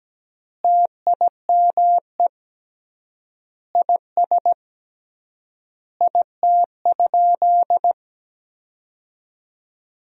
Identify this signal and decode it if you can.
Morse code, heard as a keyed tone, about 17 wpm, 700 Hz. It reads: TIME IS IT?
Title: 3x Word-Spacing - Sentences from Top 100 Words - 17wpm